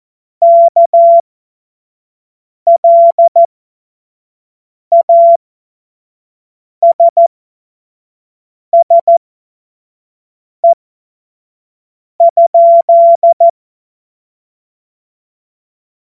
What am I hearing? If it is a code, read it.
Morse code, heard as a keyed tone, about 14 wpm, spaced as for 5 wpm, 680 Hz. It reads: KLASSE?
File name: Morsekurs / Witze / Witz 25. April 2023 25 F70